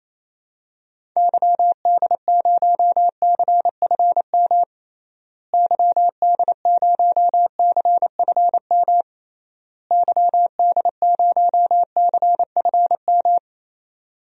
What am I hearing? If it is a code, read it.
Morse code, heard as a keyed tone, about 28 wpm, 700 Hz. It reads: YD0CFM YD0CFM YD0CFM
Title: DXCC-100 Call Signs - 6 Characters - Repeated 3x - 28wpm